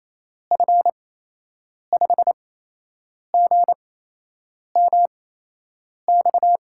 Morse code, heard as a keyed tone, about 28 wpm, spaced as for 8 wpm, 700 Hz. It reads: F5GMX
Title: USA Call Signs - 6 Characters - 8x Character Spacing - 28wpm